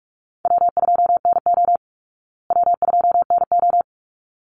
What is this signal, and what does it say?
Morse code, heard as a keyed tone, about 44 wpm, 700 Hz. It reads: W2NO W2NO
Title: Easy Call Signs Repeated 2x - 45wpm